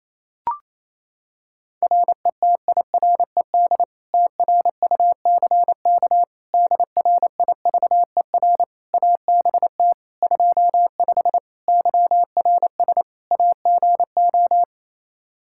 Morse code, heard as a keyed tone, about 28 wpm, 700 Hz. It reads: RETIRED TRUCK DRIVER ABT 25 YRS AGO
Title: Practice QSO - Exchange 3 - More Info - 28wpm